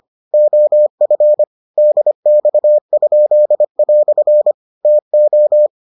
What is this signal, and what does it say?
Morse code, heard as a keyed tone, about 25 wpm, 600 Hz. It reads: OF DX?" TO